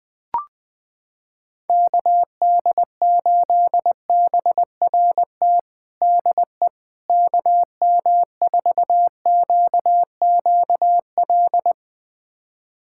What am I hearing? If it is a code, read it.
Morse code, heard as a keyed tone, about 20 wpm, 700 Hz. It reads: KD8BRT DE KM4QQL
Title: USA Callsign DE Callsign 20wpm